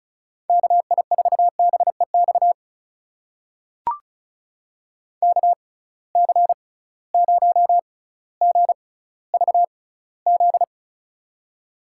Morse code, heard as a keyed tone, about 34 wpm, spaced as for 18 wpm, 700 Hz. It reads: KI4BEX KC0GVZ
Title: USA Call Signs - 6 Characters - 6x Character Spacing - 35wpm